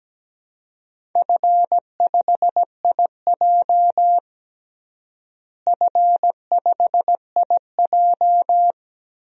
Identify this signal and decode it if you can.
Morse code, heard as a keyed tone, about 17 wpm, 700 Hz. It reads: F5IJ F5IJ
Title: DXCC-100 Call Signs - 4 Characters - Repeated 2x - 17wpm